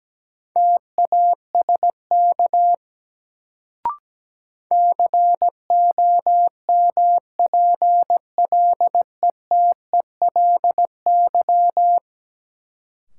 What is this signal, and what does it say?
Morse code, heard as a keyed tone, about 17 wpm, 700 Hz. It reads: TASK COMPLETELY
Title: Sentences from Top 1100 Words - New Word Drill Alt - 17wpm